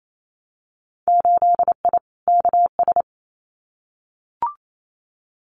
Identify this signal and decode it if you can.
Morse code, heard as a keyed tone, about 28 wpm, 700 Hz. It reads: 8I KS
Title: Winter Field Day Exchanges Encore 28wpm